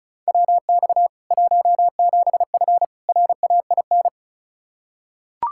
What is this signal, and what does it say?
Morse code, heard as a keyed tone, about 36 wpm, 700 Hz. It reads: WX 17F RAIN
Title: K1USN SST QSO Practice 35wpm